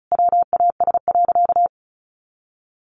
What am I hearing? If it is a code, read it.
Morse code, heard as a keyed tone, about 34 wpm, 700 Hz. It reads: WAS.
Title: Sentences from Top 1200 Words 35wpm